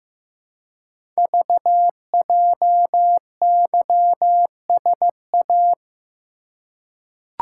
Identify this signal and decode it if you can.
Morse code, heard as a keyed tone, about 15 wpm, 700 Hz. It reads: VJYSA
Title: Code Groups - 5 Letters - 15wpm